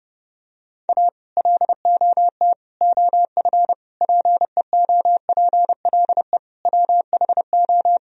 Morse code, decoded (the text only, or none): A LOT OF PEOPLE WHO